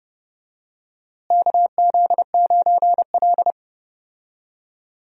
Morse code, heard as a keyed tone, about 30 wpm, 700 Hz. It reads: KZ9L
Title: USA Call Signs - 4 Characters - Repeated 2x - 30wpm